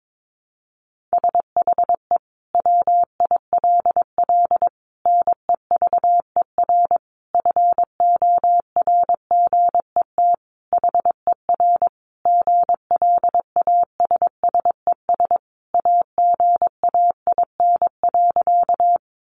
Morse code, heard as a keyed tone, about 22 wpm, 700 Hz. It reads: SHE WILL NEVER FORGET HER GLASSES AGAIN.
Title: Additional Easy Phrases 22wpm